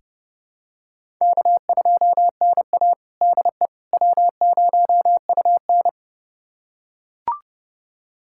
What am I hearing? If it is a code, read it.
Morse code, heard as a keyed tone, about 32 wpm, 700 Hz. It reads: K2NA DE W0UN
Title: USA Callsign DE Callsign - 4 Characters - 30wpm